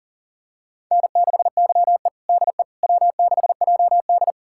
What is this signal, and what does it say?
Morse code, heard as a keyed tone, about 40 wpm, 700 Hz. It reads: N6YE DE W6JD
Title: USA Callsign DE Callsign - 4 Characters - 40wpm